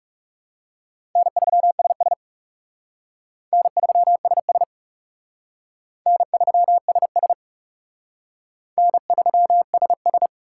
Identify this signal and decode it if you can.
Morse code, heard as a keyed tone, about 39 wpm, 700 Hz. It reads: N3SS N3SS N3SS N3SS